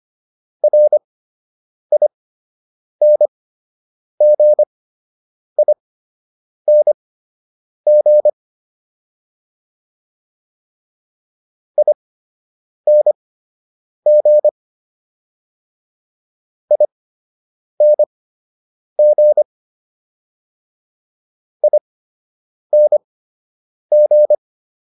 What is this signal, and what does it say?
Morse code, heard as a keyed tone, about 25 wpm, spaced as for 8 wpm, 600 Hz. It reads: RINGING ING ING ING